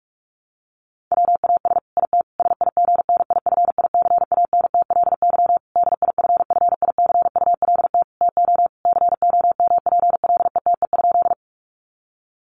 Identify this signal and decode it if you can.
Morse code, heard as a keyed tone, about 44 wpm, 700 Hz. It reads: WAS IT SIGNIFICANTLY DIFFICULT TO COMPLETE?
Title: Sentences from Top 1500 Words 45wpm